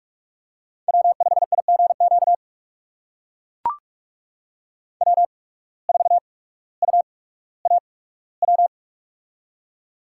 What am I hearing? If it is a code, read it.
Morse code, heard as a keyed tone, about 42 wpm, 690 Hz. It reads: W 5 I G Q W 4 U A W
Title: USA Call Signs - 8x Character Spacing - 45wpm